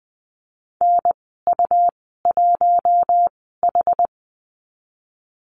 Morse code, heard as a keyed tone, about 20 wpm, spaced as for 15 wpm, 700 Hz. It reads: NU1H
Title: 122 Lesson 20g - Number 3 - Call Signs 20wpm